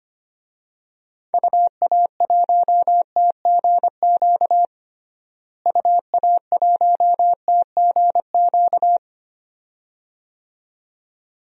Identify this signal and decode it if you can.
Morse code, heard as a keyed tone, about 25 wpm, 700 Hz. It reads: UA1TGQ UA1TGQ
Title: DXCC-100 Call Signs - 6 Characters - Repeated 2x - 25wpm